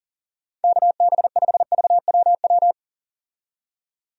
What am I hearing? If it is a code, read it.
Morse code, heard as a keyed tone, about 40 wpm, 700 Hz. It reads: KB5VWW
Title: USA Call Signs - 6 Characters - 40wpm